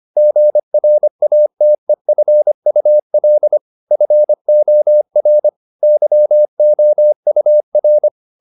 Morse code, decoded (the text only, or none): GRATEFUL FOR YOUR